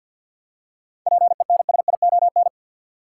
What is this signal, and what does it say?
Morse code, heard as a keyed tone, about 50 wpm, 700 Hz. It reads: PENSION
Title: Seven Letter Words - 50wpm